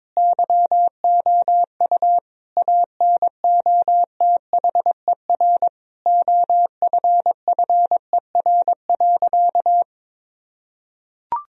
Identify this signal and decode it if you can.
Morse code, heard as a keyed tone, about 22 wpm, 700 Hz. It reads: YOU ANOTHER OFFER.
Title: Sentences from Top 1000 Words - Review Part 4 - 22wpm